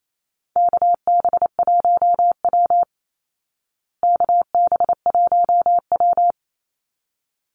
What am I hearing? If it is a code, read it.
Morse code, heard as a keyed tone, about 28 wpm, 690 Hz. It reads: KB1W KB1W
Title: Call Signs Repeated 3x - 28wpm